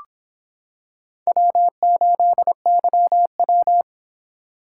Morse code, heard as a keyed tone, about 25 wpm, 700 Hz. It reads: W8YW